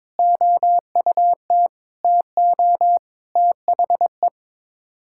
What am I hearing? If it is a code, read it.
Morse code, heard as a keyed tone, about 22 wpm, 700 Hz. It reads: OUT TO THE
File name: practice / 22wpm / Sets of 8 Words - 22wpm